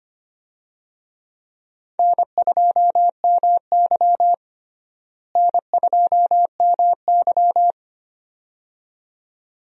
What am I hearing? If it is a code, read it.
Morse code, heard as a keyed tone, about 25 wpm, 700 Hz. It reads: N2MY N2MY
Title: USA Call Signs - 4 Characters - Repeated 2x - 25wpm